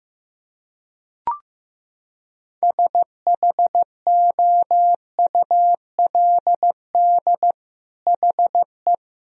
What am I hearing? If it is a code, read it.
Morse code, heard as a keyed tone, about 15 wpm, 700 Hz. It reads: SHOULD HE
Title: Sentences from Top 1000 Words - Review Part 4 - 15wpm